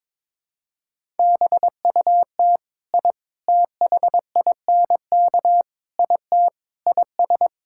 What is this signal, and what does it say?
Morse code, heard as a keyed tone, about 22 wpm, 700 Hz. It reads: BUT I THINK IT IS
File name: practice / 22wpm / Sentences with 4-Word Phrases - 22wpm